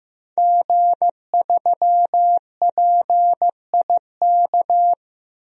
Morse code, heard as a keyed tone, about 15 wpm, 700 Hz. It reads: G3PIK
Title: DXCC-100 Call Signs - 5 Characters - 15wpm